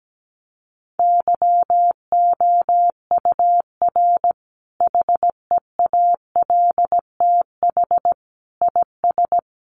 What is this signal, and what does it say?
Morse code, heard as a keyed tone, about 17 wpm, 700 Hz. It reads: YOUR HEALTH IS
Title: Sentences from Top 600 Words 17wpm